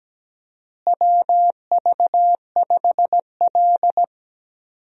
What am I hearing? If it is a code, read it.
Morse code, heard as a keyed tone, about 17 wpm, 700 Hz. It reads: WV5L
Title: USA Call Signs - 4 Characters - Repeated 2x - 17wpm